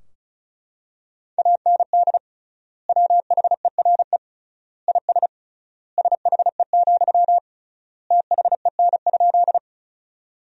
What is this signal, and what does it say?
Morse code, heard as a keyed tone, about 36 wpm, 700 Hz. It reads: AND WHERE IS SHE, THEN?
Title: 3x Word-Spacing - Sentences from Top 200 Words - 35wpm